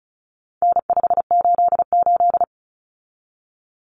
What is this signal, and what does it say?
Morse code, heard as a keyed tone, about 34 wpm, 700 Hz. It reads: N588
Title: Common Cut Numbers - 4-Digit 35wpm